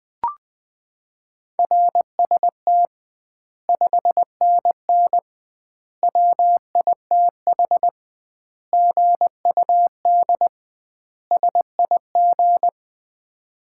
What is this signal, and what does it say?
Morse code, heard as a keyed tone, about 20 wpm, 700 Hz. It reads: RST 5NN WITH GUD SIG
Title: Common QSO Phrases - 2x Word-Spacing - 20wpm